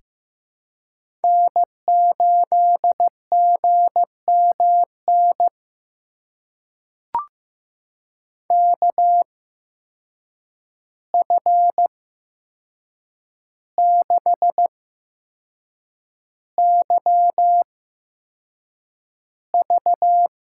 Morse code, heard as a keyed tone, about 15 wpm, spaced as for 6 wpm, 700 Hz. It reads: N8GMN KF6YV
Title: USA Call Signs - 5 Characters - 8x Character Spacing - 15wpm